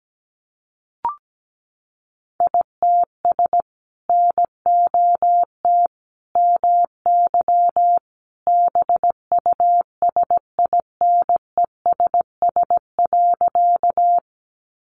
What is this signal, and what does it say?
Morse code, heard as a keyed tone, about 17 wpm, 700 Hz. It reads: ITS NOT MY BUSINESS.
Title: Sentences from Top 600 Words 17wpm